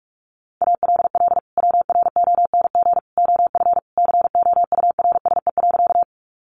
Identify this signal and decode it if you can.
Morse code, heard as a keyed tone, about 44 wpm, 700 Hz. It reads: ALL WRONG OF COURSE.
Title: Sentences from Top 900 Words 45wpm